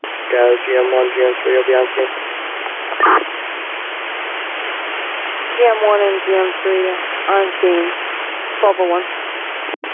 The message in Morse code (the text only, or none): none